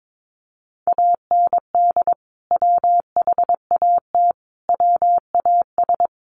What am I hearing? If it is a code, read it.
Morse code, heard as a keyed tone, about 22 wpm, 700 Hz. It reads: AND WHAT WAS
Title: Sentences from Top 1500 Words 22wpm